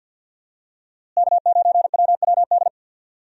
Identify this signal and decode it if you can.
Morse code, heard as a keyed tone, about 46 wpm, 700 Hz. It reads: K9WWD
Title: USA Call Signs - 5 Characters - Repeated 3x - 50wpm